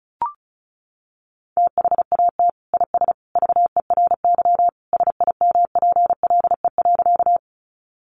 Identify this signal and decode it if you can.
Morse code, heard as a keyed tone, about 34 wpm, 700 Hz. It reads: THAT IS VERY SIMPLE.
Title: Sentences from Top 400 Words 35wpm